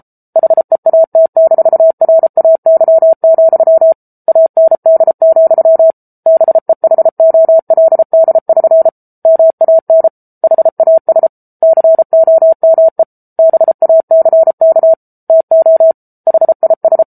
Morse code, heard as a keyed tone, about 34 wpm, 650 Hz. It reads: HEAT-RAY, AND, BEHOLD<SN> MAN HAS COME BACK TO HIS
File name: Book Two / 07-Chapter 7 - 33 WPM